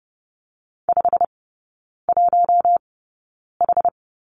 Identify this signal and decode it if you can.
Morse code, heard as a keyed tone, about 30 wpm, spaced as for 9 wpm, 700 Hz. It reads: 51H
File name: Lesson 16e - Letter c - Three Characters 30wpm